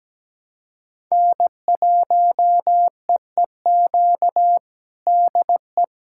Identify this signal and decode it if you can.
Morse code, heard as a keyed tone, about 17 wpm, 700 Hz. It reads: N1EEQ DE
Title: USA Callsign DE Callsign - 5 Characters - 17wpm